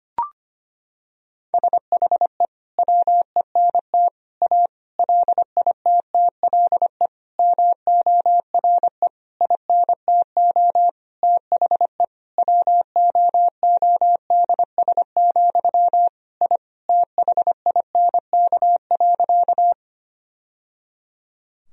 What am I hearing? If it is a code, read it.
Morse code, heard as a keyed tone, about 25 wpm, 700 Hz. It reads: SHE WENT A LITTLE MORE INTO THE WOODS, I THINK.
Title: Sentences from Top 1000 Words - Review Part 2 - 25wpm